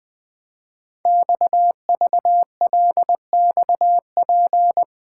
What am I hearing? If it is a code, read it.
Morse code, heard as a keyed tone, about 20 wpm, 700 Hz. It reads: XVLXP
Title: Code Groups - 5 Letters - 20wpm